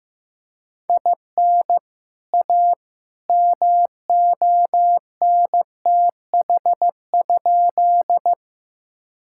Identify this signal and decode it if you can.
Morse code, heard as a keyed tone, about 15 wpm, 700 Hz. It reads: IN A MONTH?
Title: Sentences from Top 1000 Words - Review Part 5 - 15wpm